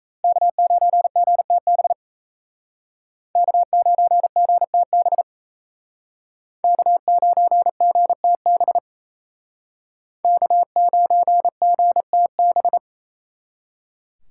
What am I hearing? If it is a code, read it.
Morse code, heard as a keyed tone, about 36 wpm, 700 Hz. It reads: K9GTB K9GTB K9GTB K9GTB